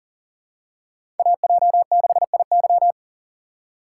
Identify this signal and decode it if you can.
Morse code, heard as a keyed tone, about 40 wpm, 700 Hz. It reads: AJ6IY
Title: USA Call Signs - 6x Character Spacing - 40wpm